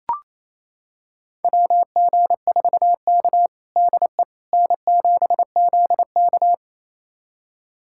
Morse code, heard as a keyed tone, about 28 wpm, 700 Hz. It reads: WG4K DE N7ZK
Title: USA Callsign DE Callsign - 4 Characters - 28wpm